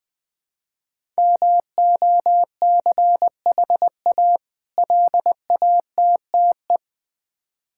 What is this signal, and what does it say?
Morse code, heard as a keyed tone, about 20 wpm, 700 Hz. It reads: MOCHA LATTE
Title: Difficult Words with Hints - 20wpm